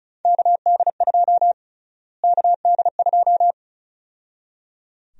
Morse code, heard as a keyed tone, about 34 wpm, 700 Hz. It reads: KD2 KD2
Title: USA Call Sign Prefixes - Repeated 2x - 35wpm